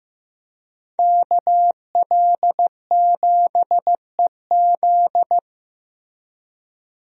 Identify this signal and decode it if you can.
Morse code, heard as a keyed tone, about 15 wpm, 700 Hz. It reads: KL7EZ
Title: DXCC-100 Call Signs - 5 Characters - Repeated 3x - 15wpm